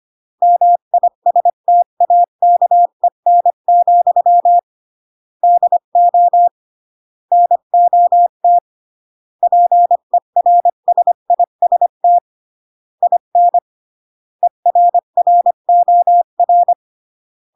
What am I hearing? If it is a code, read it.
Morse code, heard as a keyed tone, about 25 wpm, 700 Hz. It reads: MISTAKEN, DO NOT PERSIST IN ERROR